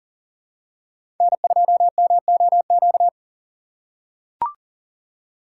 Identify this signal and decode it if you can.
Morse code, heard as a keyed tone, about 38 wpm, 700 Hz. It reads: N2MOQ
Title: USA Call Signs - 4x Character Spacing - 40wpm